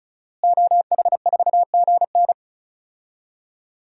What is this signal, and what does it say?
Morse code, heard as a keyed tone, about 36 wpm, 700 Hz. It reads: OH4GN
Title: DXCC-100 Call Signs - 5 Characters - Repeated 3x - 35wpm